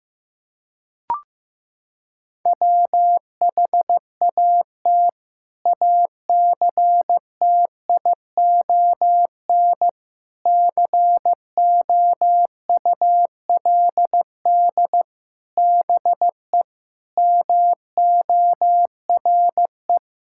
Morse code, heard as a keyed tone, about 15 wpm, 700 Hz. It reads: WHAT ACTION COULD BE MORE
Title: Sentences from Top 1000 Words - Review Part 6 - 15wpm